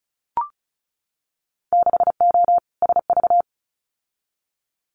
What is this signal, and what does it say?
Morse code, heard as a keyed tone, about 36 wpm, 700 Hz. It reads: 6O SV
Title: Winter Field Day Exchanges Encore 35wpm